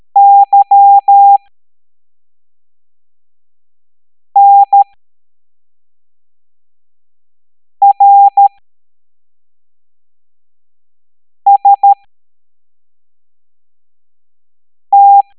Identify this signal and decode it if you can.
Morse code, heard as a keyed tone, about 13 wpm, spaced as for 3 wpm, 800 Hz. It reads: YNRST
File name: hamdata / L05-EX01-SVY